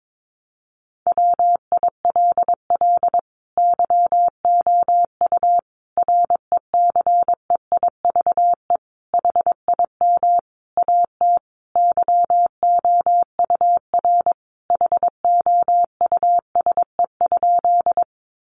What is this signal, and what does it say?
Morse code, heard as a keyed tone, about 22 wpm, 700 Hz. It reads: WILL YOU RECEIVE HIM AT YOUR HOUSE?
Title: Sentences from Top 1000 Words 22wpm